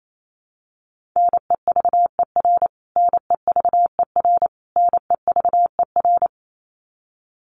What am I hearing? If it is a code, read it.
Morse code, heard as a keyed tone, about 28 wpm, 700 Hz. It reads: NEVER NEVER NEVER
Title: Top 500 Words Repeated 28wpm